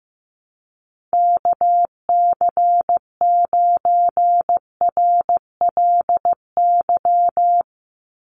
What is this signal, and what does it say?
Morse code, heard as a keyed tone, about 15 wpm, 700 Hz. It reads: KC9RLY
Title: USA Call Signs 15wpm